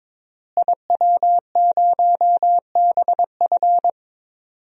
Morse code, heard as a keyed tone, about 22 wpm, 700 Hz. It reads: IW0BF
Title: DXCC-100 Call Signs - 5 Characters - Repeated 2x - 22wpm